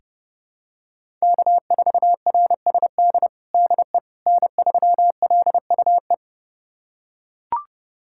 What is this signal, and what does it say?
Morse code, heard as a keyed tone, about 30 wpm, 700 Hz. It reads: K4RSD DE N3LUE
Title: USA Callsign DE Callsign - 5 Characters - 30wpm